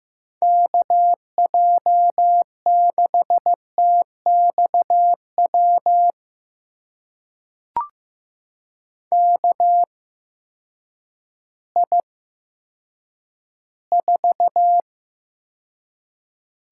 Morse code, heard as a keyed tone, about 15 wpm, spaced as for 9 wpm, 700 Hz. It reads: KJ6TXW KI4
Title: USA Call Signs - 6 Characters - 8x Character Spacing - 15wpm